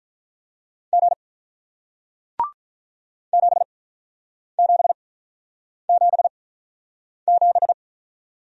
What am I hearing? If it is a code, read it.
Morse code, heard as a keyed tone, about 48 wpm, spaced as for 9 wpm, 700 Hz. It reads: G 7777